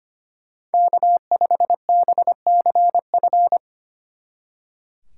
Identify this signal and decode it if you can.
Morse code, heard as a keyed tone, about 32 wpm, 700 Hz. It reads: K5BCF